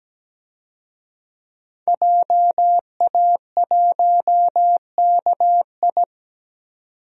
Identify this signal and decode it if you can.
Morse code, heard as a keyed tone, about 17 wpm, 700 Hz. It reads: JA1KI
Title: DXCC-100 Call Signs - 5 Characters - 17wpm